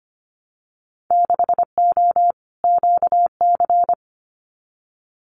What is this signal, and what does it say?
Morse code, heard as a keyed tone, about 25 wpm, 690 Hz. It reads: 6O QC
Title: Winter Field Day Exchanges Encore 25wpm